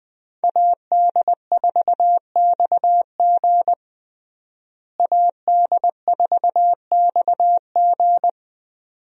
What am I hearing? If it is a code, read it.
Morse code, heard as a keyed tone, about 20 wpm, 700 Hz. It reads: AD4XG AD4XG
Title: USA Call Signs - 5 Characters - Repeated 2x - 20wpm